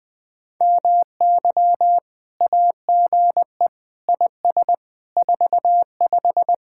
Morse code, heard as a keyed tone, about 20 wpm, 700 Hz. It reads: MY AGE IS 45
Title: Speed-Racing Common QSO Phrases - 20wpm